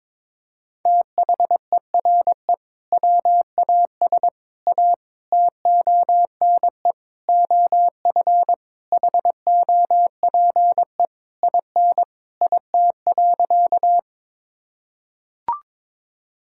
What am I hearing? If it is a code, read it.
Morse code, heard as a keyed tone, about 22 wpm, 700 Hz. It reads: THERE WAS A TONE OF HOPE IN IT.
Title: Sentences from Top 1600 Words 22wpm